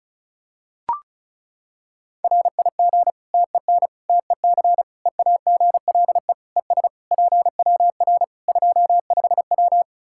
Morse code, heard as a keyed tone, about 36 wpm, 700 Hz. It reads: RIG TEN TEC EAGLE ES PWR 25W
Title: Practice QSO - Exchange 2 - Info - 35wpm